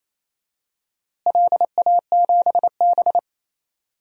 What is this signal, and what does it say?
Morse code, heard as a keyed tone, about 28 wpm, 700 Hz. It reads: LA7B
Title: DXCC-100 Call Signs - 4 Characters - Repeated 2x - 28wpm